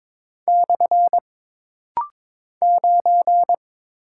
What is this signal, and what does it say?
Morse code, heard as a keyed tone, about 22 wpm, 700 Hz. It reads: / 9
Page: This rapid-fire timing is geared to push your ICR (Instant Character Recognition) skills.